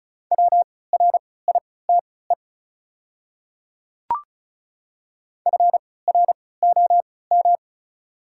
Morse code, heard as a keyed tone, about 34 wpm, spaced as for 20 wpm, 700 Hz. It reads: WRITE FROM
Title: 3x Character Spacing - Top 100 Words - 35wpm